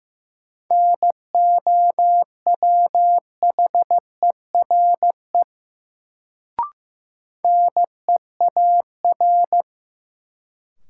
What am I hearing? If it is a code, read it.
Morse code, heard as a keyed tone, about 15 wpm, 700 Hz. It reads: NOWHERE NEAR
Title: Sets of 5 Words Encore - New Word Drill - 15wpm